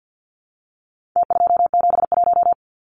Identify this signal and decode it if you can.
Morse code, heard as a keyed tone, about 48 wpm, 700 Hz. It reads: T271
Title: Numbers - Contest Style - 4-Digit 50wpm